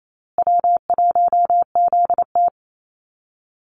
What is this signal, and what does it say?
Morse code, heard as a keyed tone, about 28 wpm, 700 Hz. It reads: W1ZT
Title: Call Signs Repeated 2x - 28wpm